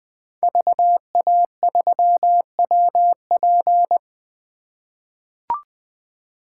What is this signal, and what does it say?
Morse code, heard as a keyed tone, about 20 wpm, 700 Hz. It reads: VA3WP
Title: 2023 POTA Activator Call Signs - 20wpm